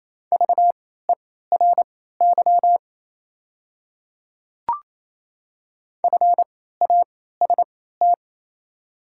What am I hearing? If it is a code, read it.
Morse code, heard as a keyed tone, about 28 wpm, spaced as for 16 wpm, 700 Hz. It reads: VERY FAST
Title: 3x Character Spacing - Top 500 Words - 28wpm